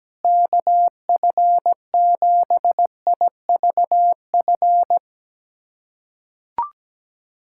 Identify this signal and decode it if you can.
Morse code, heard as a keyed tone, about 17 wpm, 700 Hz. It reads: KF7IVF
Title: USA Callsign DE Callsign 17wpm